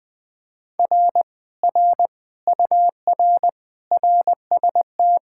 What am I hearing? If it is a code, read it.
Morse code, heard as a keyed tone, about 21 wpm, 700 Hz. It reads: R R UR RST
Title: Speed-Racing Common QSO Phrases - 20wpm